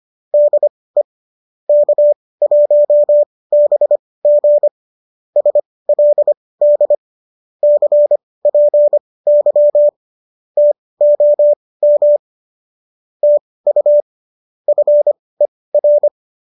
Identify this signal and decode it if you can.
Morse code, heard as a keyed tone, about 25 wpm, spaced as for 18 wpm, 590 Hz. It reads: DE K1BG SLD CPY TOM TU FER
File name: qso201 18